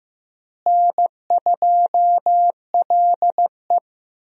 Morse code, heard as a keyed tone, about 15 wpm, 700 Hz. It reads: N2LE
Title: USA Call Signs - 4 Characters - Repeated 3x - 15wpm